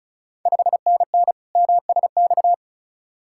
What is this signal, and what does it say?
Morse code, heard as a keyed tone, about 36 wpm, 690 Hz. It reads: 5NN MSX
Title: Ontario QSO Party Exchange 35wpm